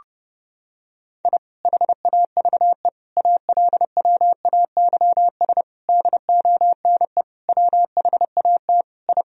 Morse code, heard as a keyed tone, about 32 wpm, 700 Hz. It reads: I HAVE ALWAYS DONE WHAT I
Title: Sentences from Top 1000 Words - Review Part 2 - 30wpm